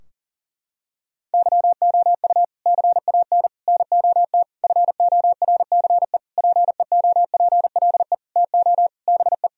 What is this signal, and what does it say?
Morse code, heard as a keyed tone, about 40 wpm, 700 Hz. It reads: YOU CAN NOT FORCE PEOPLE TO BE
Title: Sentences from Top 1000 Words - Review Part 3 - 40wpm